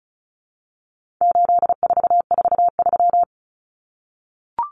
Morse code, heard as a keyed tone, about 34 wpm, 700 Hz. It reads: 8443
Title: Common Cut Numbers - 4-Digit 35wpm